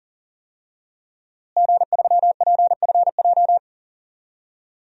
Morse code, heard as a keyed tone, about 39 wpm, 700 Hz. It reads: G3PFJ
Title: DXCC-100 Call Signs - 5 Characters - Repeated 2x - 40wpm